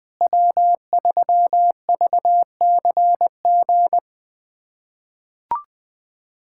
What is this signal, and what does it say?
Morse code, heard as a keyed tone, about 21 wpm, 700 Hz. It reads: W3VCG
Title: USA Callsign DE Callsign 20wpm